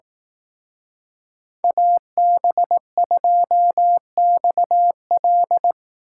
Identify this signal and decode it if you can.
Morse code, heard as a keyed tone, about 18 wpm, 700 Hz. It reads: AB2XL
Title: Speed Racing - USA 2x2 Call Signs 15wpm